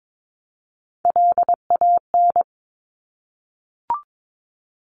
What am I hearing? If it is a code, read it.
Morse code, heard as a keyed tone, about 22 wpm, 700 Hz. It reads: LAN
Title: Top Three Letter Combinations in Words 22wpm